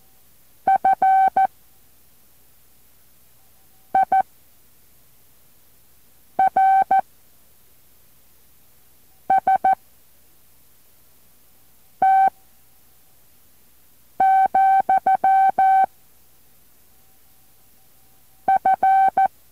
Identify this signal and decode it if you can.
Morse code, heard as a keyed tone, about 14 wpm, 770 Hz. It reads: F I R S T , F